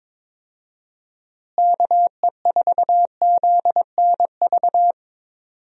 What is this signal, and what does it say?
Morse code, heard as a keyed tone, about 23 wpm, 700 Hz. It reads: KE4ZNV
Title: USA Call Signs - 6 Characters - Repeated 3x - 22wpm